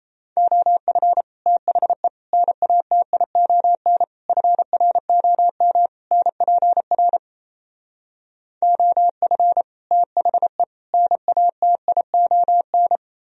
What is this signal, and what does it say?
Morse code, heard as a keyed tone, about 38 wpm, 700 Hz. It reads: OF THE NATION FROM NPR OF THE NATION